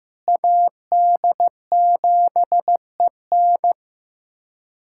Morse code, heard as a keyed tone, about 16 wpm, 700 Hz. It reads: AD7EN
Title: USA Callsign DE Callsign - 5 Characters - 15wpm